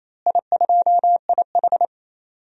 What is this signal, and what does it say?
Morse code, heard as a keyed tone, about 28 wpm, 700 Hz. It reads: I2IH
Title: DXCC-100 Call Signs - 4 Characters - Repeated 2x - 28wpm